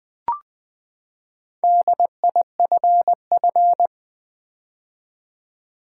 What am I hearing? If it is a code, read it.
Morse code, heard as a keyed tone, about 20 wpm, 700 Hz. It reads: DIFF
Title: Build A Word - 4 Letters At A Time - 11-Letter Words - 20wpm